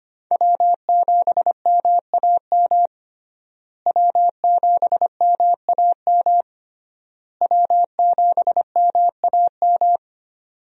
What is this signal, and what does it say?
Morse code, heard as a keyed tone, about 25 wpm, 700 Hz. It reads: W7MAM W7MAM W7MAM
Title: USA Call Signs - 5 Characters - Repeated 3x - 25wpm